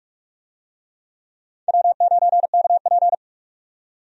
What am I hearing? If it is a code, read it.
Morse code, heard as a keyed tone, about 46 wpm, 700 Hz. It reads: W9KP
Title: USA Call Signs - 4 Characters - 45wpm